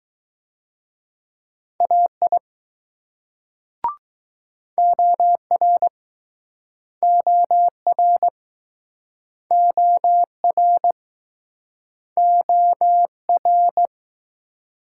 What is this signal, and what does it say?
Morse code, heard as a keyed tone, about 21 wpm, 700 Hz. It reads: AI OR OR OR OR